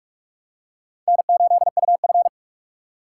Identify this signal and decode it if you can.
Morse code, heard as a keyed tone, about 44 wpm, 700 Hz. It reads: N8UF
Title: USA Call Signs - 4 Characters - 2x Character Spacing - 45wpm